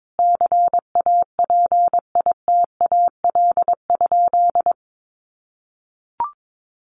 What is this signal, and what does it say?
Morse code, heard as a keyed tone, about 22 wpm, 700 Hz. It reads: CAPITAL?
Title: Sentences from Top 1100 Words 22wpm